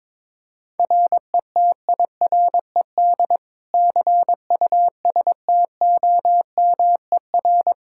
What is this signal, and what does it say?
Morse code, heard as a keyed tone, about 22 wpm, 700 Hz. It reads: RETIRED CUSTOMER
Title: Practice QSO - Exchange 3 - More Info - 22wpm